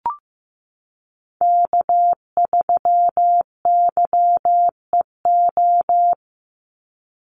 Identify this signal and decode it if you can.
Morse code, heard as a keyed tone, about 15 wpm, 700 Hz. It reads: K3YEO
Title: Call Signs Repeated 2x - 15wpm